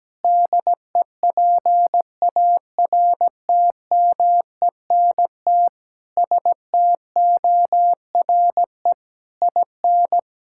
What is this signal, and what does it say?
Morse code, heard as a keyed tone, about 17 wpm, 700 Hz. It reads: DEPARTMENT STORE IN
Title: Sentences from Top 1000 Words - Review Part 5 - 17wpm